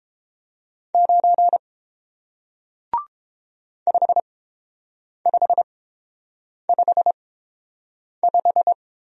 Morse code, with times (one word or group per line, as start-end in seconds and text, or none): 0.9-1.6 9
3.9-8.7 5555